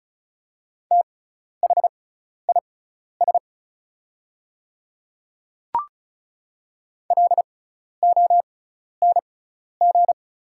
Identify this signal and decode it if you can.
Morse code, heard as a keyed tone, about 35 wpm, spaced as for 12 wpm, 690 Hz. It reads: THIS LONG
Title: 6x Character Spacing - Top 100 Words - 35wpm